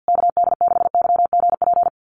Morse code, heard as a keyed tone, about 50 wpm, 700 Hz. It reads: KD6YGP
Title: USA Call Signs Repeated 3x 50wpm